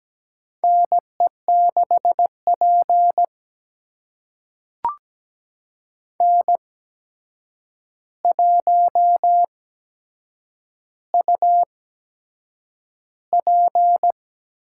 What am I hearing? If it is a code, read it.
Morse code, heard as a keyed tone, about 17 wpm, spaced as for 7 wpm, 700 Hz. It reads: NE6P N1UP
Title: USA Call Signs - 8x Character Spacing - 17wpm